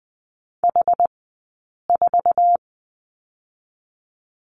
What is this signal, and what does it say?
Morse code, heard as a keyed tone, about 20 wpm, 700 Hz. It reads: H 4
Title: 126 Lesson 21d - Number 6 - Two Characters 20wpm